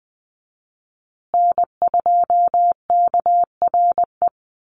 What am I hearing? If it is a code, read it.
Morse code, heard as a keyed tone, about 20 wpm, 700 Hz. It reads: N2KRE
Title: USA Call Signs Repeated 2x 20wpm